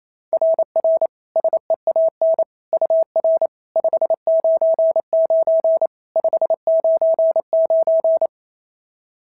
Morse code, heard as a keyed tone, about 25 wpm, 650 Hz. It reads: RR SEAN UR 599 599
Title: SOTA QSO Practice 28wpm